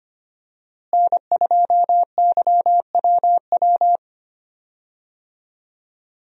Morse code, heard as a keyed tone, about 25 wpm, 700 Hz. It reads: N2YWW
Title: USA Call Signs - 5 Characters - Repeated 2x - 25wpm